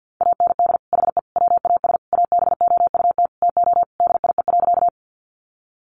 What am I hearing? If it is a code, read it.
Morse code, heard as a keyed tone, about 50 wpm, 700 Hz. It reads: AND HE WAS ABOUT TO DIE.
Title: Sentences from Top 1300 Words 50wpm